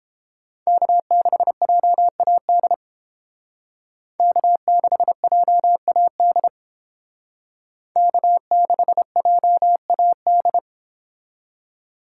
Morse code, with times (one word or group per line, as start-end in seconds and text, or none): 0.7-2.7 K6JAD
4.2-6.5 K6JAD
8.0-10.6 K6JAD